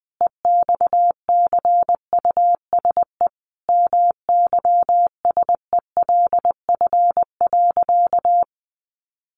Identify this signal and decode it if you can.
Morse code, heard as a keyed tone, about 20 wpm, 700 Hz. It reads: EXCUSE MYSELF.
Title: Sentences from Top 1300 Words 20wpm